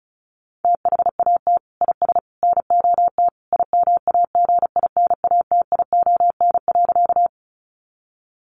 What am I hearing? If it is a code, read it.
Morse code, heard as a keyed tone, about 34 wpm, 700 Hz. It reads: THAT IS NOT IMAGINATION.
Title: Sentences from Top 1300 Words 35wpm